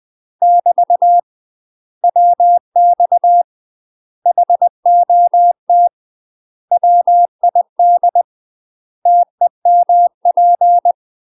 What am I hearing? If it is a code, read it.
Morse code, heard as a keyed tone, about 20 wpm, 700 Hz. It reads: <BT> WX HOT WID TEMP
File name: UK MASTER QSOs TRAINING / 20WPM / UK MASTER QSOs 0043